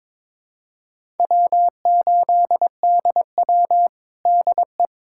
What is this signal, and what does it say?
Morse code, heard as a keyed tone, about 22 wpm, 700 Hz. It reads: W8DW DE
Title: USA Callsign DE Callsign - 4 Characters - 22wpm